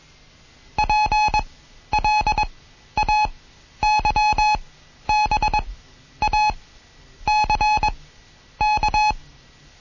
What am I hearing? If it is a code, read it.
Morse code, heard as a keyed tone, about 21 wpm, spaced as for 11 wpm, 850 Hz. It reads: PLAYBACK
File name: Morsen / Kurs A 20